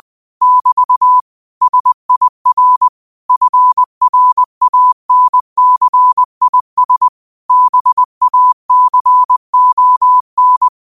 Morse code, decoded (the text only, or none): <BT> SIR FRANCIS BACON